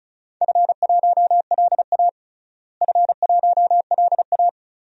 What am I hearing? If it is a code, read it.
Morse code, heard as a keyed tone, about 36 wpm, 700 Hz. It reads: F1LA F1LA
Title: DXCC-100 Call Signs - 4 Characters - Repeated 3x - 35wpm